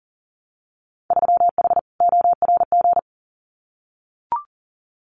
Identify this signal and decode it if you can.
Morse code, heard as a keyed tone, about 40 wpm, 700 Hz. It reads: 3H ORG
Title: Winter Field Day Exchanges Encore 40wpm